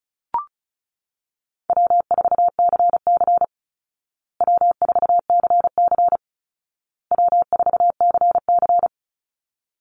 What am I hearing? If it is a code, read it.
Morse code, heard as a keyed tone, about 34 wpm, 700 Hz. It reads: W4CC W4CC W4CC
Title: Easy Call Signs Repeated 3x - 35wpm